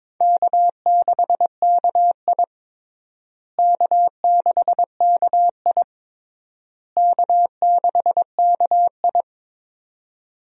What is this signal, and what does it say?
Morse code, heard as a keyed tone, about 22 wpm, 700 Hz. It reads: K6KI K6KI K6KI
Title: USA Call Signs - 4 Characters - Repeated 3x - 22wpm